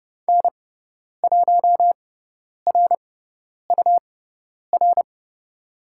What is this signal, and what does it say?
Morse code, heard as a keyed tone, about 30 wpm, spaced as for 10 wpm, 700 Hz. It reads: N1RUR
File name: Lesson 17g - Letter u - Call Signs 30wpm